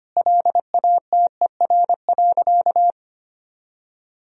Morse code, heard as a keyed tone, about 25 wpm, 700 Hz. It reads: LATER.
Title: Sentences from Top 1000 Words - Review Part 4 - 25wpm